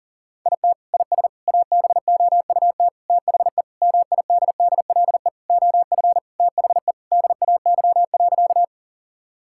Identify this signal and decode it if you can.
Morse code, heard as a keyed tone, about 40 wpm, 700 Hz. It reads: IT IS ABOUT THE MIDDLE OF THE DAY.
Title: Sentences from Top 1000 Words - Review Part 6 - 40wpm